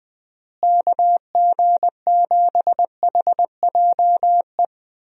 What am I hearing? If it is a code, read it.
Morse code, heard as a keyed tone, about 21 wpm, 700 Hz. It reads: KG7HJE
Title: USA Call Signs - 6 Characters - Repeated 3x - 20wpm